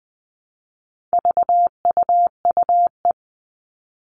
Morse code, heard as a keyed tone, about 20 wpm, 700 Hz. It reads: VUUE
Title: All Cut Numbers - 4-Digit 20wpm